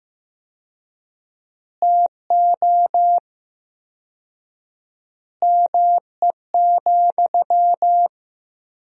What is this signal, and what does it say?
Morse code, heard as a keyed tone, about 16 wpm, 700 Hz. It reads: TO ME,
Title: 4x Word-Spacing - Sentences from Top 500 Words - 15wpm